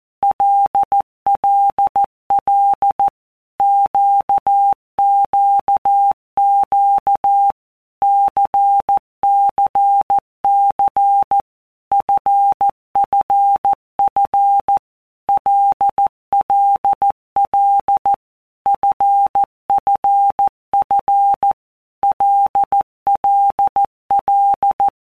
Morse code, heard as a keyed tone, about 14 wpm, 800 Hz. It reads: LLL QQQ CCC FFF LLL FFF LLL
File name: Basic Letter Groups By WPM's / Groups@13 / LFCYQ@13